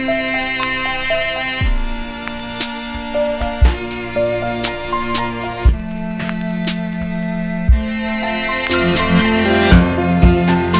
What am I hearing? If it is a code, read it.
Dial-up modem handshake.